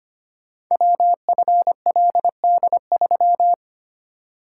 Morse code, heard as a keyed tone, about 25 wpm, 700 Hz. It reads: WFLD3
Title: Code Groups - 5 Letter:Numbers - 25wpm